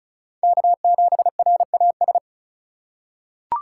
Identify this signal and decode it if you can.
Morse code, heard as a keyed tone, about 34 wpm, 700 Hz. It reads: K7RAS
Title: USA Call Signs - 5 Characters - 3x Character Spacing - 35wpm